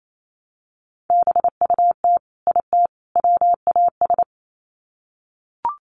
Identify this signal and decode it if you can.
Morse code, heard as a keyed tone, about 28 wpm, 700 Hz. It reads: BUT IT WAS
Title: Sets of 3 Words 28wpm